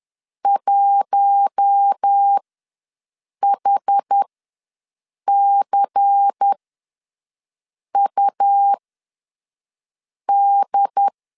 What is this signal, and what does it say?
Morse code, heard as a keyed tone, about 10 wpm, spaced as for 5 wpm, 800 Hz. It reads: 1HCUD
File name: LEZIONI / 5 LBVC / 5 LBVC ws 10wpm 3733